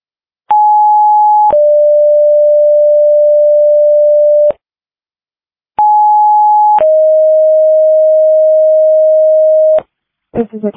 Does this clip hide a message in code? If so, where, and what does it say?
Yes: two-tone page from 0.5 to 9.8 s, heard as a one-second tone, then a three-second tone, about 840 then 590 Hz, about 840 then 620 Hz, with speech.